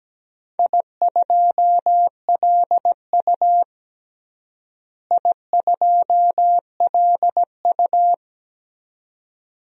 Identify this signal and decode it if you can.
Morse code, heard as a keyed tone, about 17 wpm, 700 Hz. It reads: I2LU I2LU
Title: DXCC-100 Call Signs - 4 Characters - Repeated 2x - 17wpm